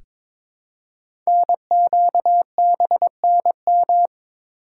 Morse code, heard as a keyed tone, about 22 wpm, 690 Hz. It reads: NQBNM
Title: Code Groups - 5 Letters - 22wpm